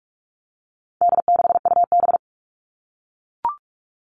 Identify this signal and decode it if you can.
Morse code, heard as a keyed tone, about 44 wpm, 700 Hz. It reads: D6UB
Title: All Cut Numbers - 4-Digit 45wpm